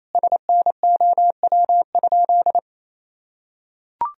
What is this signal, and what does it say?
Morse code, heard as a keyed tone, about 28 wpm, 700 Hz. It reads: SNOW?
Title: Sentences from Top 1000 Words - Review Part 3 - 28wpm